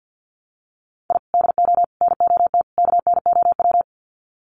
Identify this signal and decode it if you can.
Morse code, heard as a keyed tone, about 50 wpm, 700 Hz. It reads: I DO NOT KNOW
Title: Sets of 4 Words 50wpm